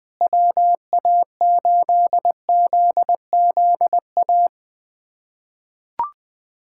Morse code, heard as a keyed tone, about 20 wpm, 700 Hz. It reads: WA8ZZA
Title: USA Callsign DE Callsign - 6 Characters - 20wpm